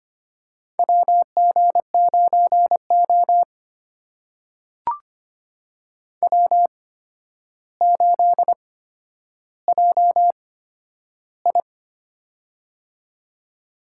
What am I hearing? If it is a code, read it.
Morse code, heard as a keyed tone, about 25 wpm, spaced as for 11 wpm, 700 Hz. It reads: WG9O W8JI
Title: USA Call Signs - 4 Characters - 8x Character Spacing - 25wpm